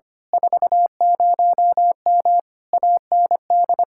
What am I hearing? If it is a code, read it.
Morse code, heard as a keyed tone, about 24 wpm, 700 Hz. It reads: 40M AND
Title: Speed-Racing Common QSO Phrases Encore - 25wpm